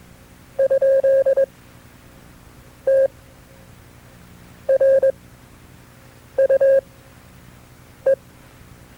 Morse code, heard as a keyed tone, about 21 wpm, spaced as for 6 wpm, 540 Hz. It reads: ? TRUE
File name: FreeFiles / CW-practice / KB6MT-tape2-side2